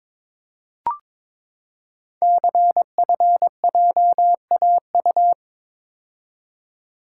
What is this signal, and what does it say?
Morse code, heard as a keyed tone, about 23 wpm, 700 Hz. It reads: CFJAU
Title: Code Groups - 5 Letters - 22wpm